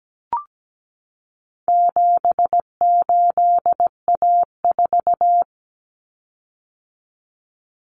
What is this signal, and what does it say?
Morse code, heard as a keyed tone, about 17 wpm, 700 Hz. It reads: 78A4
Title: Common Cut Numbers - 4-Digit 17wpm